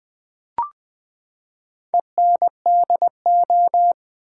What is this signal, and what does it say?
Morse code, heard as a keyed tone, about 20 wpm, 700 Hz. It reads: ENDO
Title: Build A Word - 4 Letters At A Time - 11-Letter Words - 20wpm